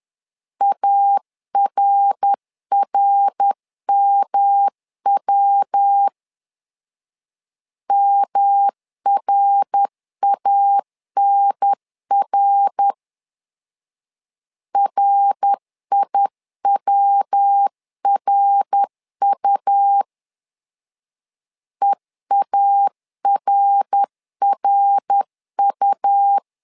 Morse code, heard as a keyed tone, about 10 wpm, 800 Hz. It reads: ARRMW MRANR RIWRU EARRU
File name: LEZIONI / 3 MWRO / 3 MWR 10wpm 1952